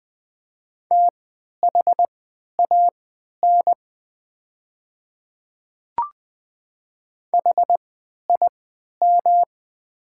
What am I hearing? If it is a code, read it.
Morse code, heard as a keyed tone, about 20 wpm, spaced as for 11 wpm, 700 Hz. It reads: THAN HIM
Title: 3x Character Spacing - Top 100 Words - 20wpm